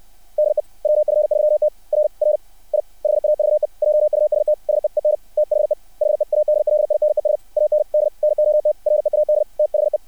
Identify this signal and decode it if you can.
Morse code, heard as a keyed tone, about 60 wpm, 600 Hz. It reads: DE WF3T R R I HAVE JUST SEEN THE HEADLINES AND NOT READ THE
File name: archived / CRSnet